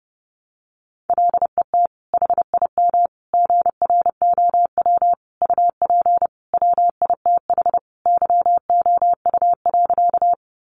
Morse code, heard as a keyed tone, about 30 wpm, 700 Hz. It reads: LET HIM GROW UP WITH YOU.
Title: Sentences from Top 300 Words 30wpm